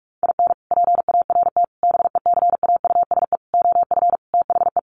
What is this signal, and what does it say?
Morse code, heard as a keyed tone, about 44 wpm, 700 Hz. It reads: IN PART BECAUSE OF THE
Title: Sets of 5 Words Encore2 45wpm